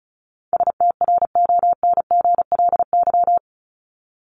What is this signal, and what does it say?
Morse code, heard as a keyed tone, about 34 wpm, 700 Hz. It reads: STRONGLY
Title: LY Suffix - Top 100 Words with Base Words 35wpm